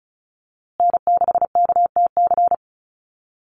Morse code, heard as a keyed tone, about 34 wpm, 700 Hz. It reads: N6KTC
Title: USA Call Signs Repeated 3x 35wpm